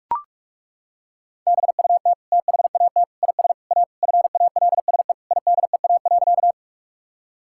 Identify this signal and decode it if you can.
Morse code, heard as a keyed tone, about 41 wpm, 700 Hz. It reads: BUT THAT IS A FALSE IDEA.
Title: Sentences from Top 1600 Words 45wpm